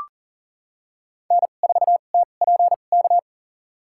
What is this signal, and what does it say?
Morse code, heard as a keyed tone, about 40 wpm, spaced as for 29 wpm, 700 Hz. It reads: N4TPK
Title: USA Call Signs - 2x Character Spacing - 40wpm